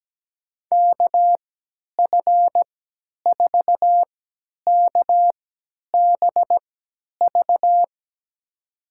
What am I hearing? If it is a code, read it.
Morse code, heard as a keyed tone, about 17 wpm, spaced as for 10 wpm, 700 Hz. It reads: KF4KBV
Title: USA Call Signs - 6 Characters - 3x Character Spacing - 17wpm